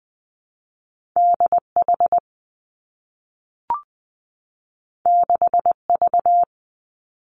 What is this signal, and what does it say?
Morse code, heard as a keyed tone, about 20 wpm, 700 Hz. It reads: D H 6 V
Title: Two Challenging Letter-Number Combinations 20wpm